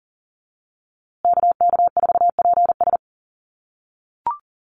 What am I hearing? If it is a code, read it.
Morse code, heard as a keyed tone, about 40 wpm, 700 Hz. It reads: KK4PS
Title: USA Call Signs Repeated 3x 40wpm